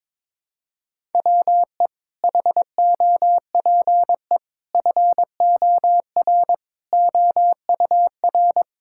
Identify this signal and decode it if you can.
Morse code, heard as a keyed tone, about 22 wpm, 700 Hz. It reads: WE HOPE FOR OUR
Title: Sentences from Top 1000 Words - Review Part 6 - 22wpm